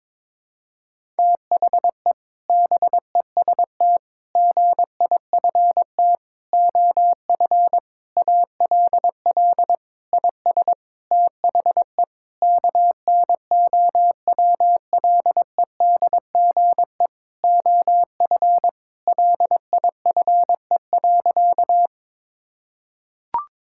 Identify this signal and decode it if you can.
Morse code, heard as a keyed tone, about 22 wpm, 700 Hz. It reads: THE BEST GIFT OF ALL IS THE KNOWLEDGE OF LIFE.
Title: Sentences from Top 1000 Words - Review Part 6 - 22wpm